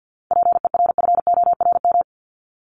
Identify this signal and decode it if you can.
Morse code, heard as a keyed tone, about 50 wpm, 700 Hz. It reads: PERFORM
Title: Top 751-1000 Words 50wpm